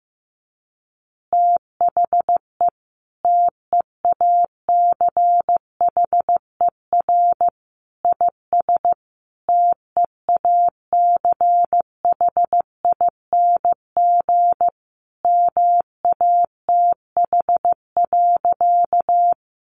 Morse code, heard as a keyed tone, about 15 wpm, 700 Hz. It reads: THE TEACHER IS TEACHING MATH.
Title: Easy Phrases 15wpm